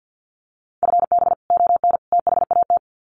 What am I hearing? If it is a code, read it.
Morse code, heard as a keyed tone, about 50 wpm, 700 Hz. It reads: FB ON THAT
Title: Common QSO Phrases 50wpm